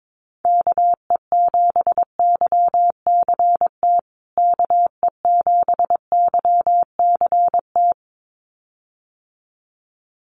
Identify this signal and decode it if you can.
Morse code, heard as a keyed tone, about 22 wpm, 700 Hz. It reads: KE7YCT KE7YCT
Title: USA Call Signs Repeated 2x 22wpm